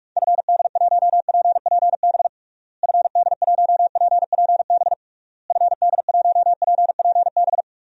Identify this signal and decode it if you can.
Morse code, heard as a keyed tone, about 46 wpm, 700 Hz. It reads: FD1PPB FD1PPB FD1PPB
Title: DXCC-100 Call Signs - 6 Characters - Repeated 3x - 45wpm